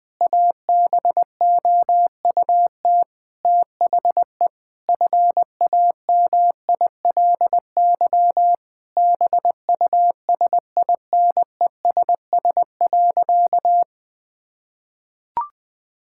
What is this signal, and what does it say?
Morse code, heard as a keyed tone, about 21 wpm, 700 Hz. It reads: ABOUT THE FAMILY BUSINESS.
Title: Sentences from Top 1000 Words - Review Part 6 - 20wpm